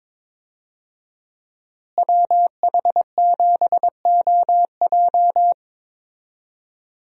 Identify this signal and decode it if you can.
Morse code, heard as a keyed tone, about 22 wpm, 700 Hz. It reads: WH7OJ
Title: USA Call Signs - 5 Characters - Repeated 3x - 22wpm